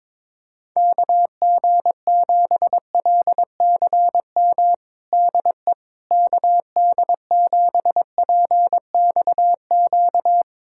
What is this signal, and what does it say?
Morse code, heard as a keyed tone, about 22 wpm, 700 Hz. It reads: KG7LCM DE KD7PXQ
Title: USA Callsign DE Callsign - 6 Characters - 22wpm